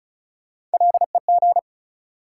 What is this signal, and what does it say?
Morse code, heard as a keyed tone, about 36 wpm, 700 Hz. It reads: LEG
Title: Three Letter Words - 35wpm